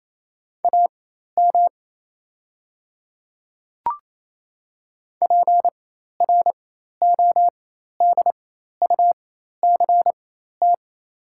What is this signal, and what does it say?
Morse code, heard as a keyed tone, about 28 wpm, spaced as for 13 wpm, 700 Hz. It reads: AM PRODUCT
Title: 4x Character Spacing - Top 500 Words - 28wpm